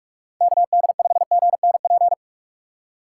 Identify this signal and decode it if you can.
Morse code, heard as a keyed tone, about 49 wpm, 700 Hz. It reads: KD5GNP